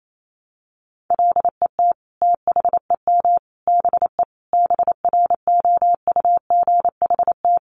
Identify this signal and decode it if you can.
Morse code, heard as a keyed tone, about 28 wpm, 700 Hz. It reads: LET THEM BE BROUGHT
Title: Sentences from Top 1300 Words 28wpm